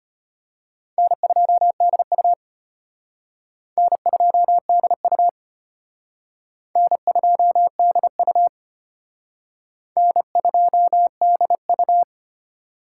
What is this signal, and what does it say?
Morse code, heard as a keyed tone, about 32 wpm, 700 Hz. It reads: N2DU N2DU N2DU N2DU